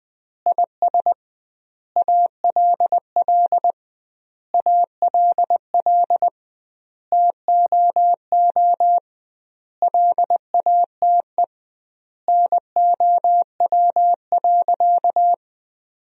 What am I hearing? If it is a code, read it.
Morse code, heard as a keyed tone, about 20 wpm, 700 Hz. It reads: IS ALL ALL TOO LATE NOW.
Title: 2x Word-Spacing - Sentences from Top 300 Words - 20wpm